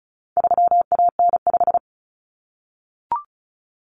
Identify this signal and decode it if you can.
Morse code, heard as a keyed tone, about 34 wpm, 700 Hz. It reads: 3AN5
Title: Common Cut Numbers - 4-Digit 35wpm